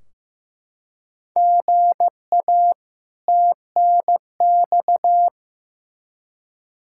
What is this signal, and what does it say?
Morse code, heard as a keyed tone, about 15 wpm, 700 Hz. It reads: GA TNX
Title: Practice QSO - Exchange 1 - Beginning - 15wpm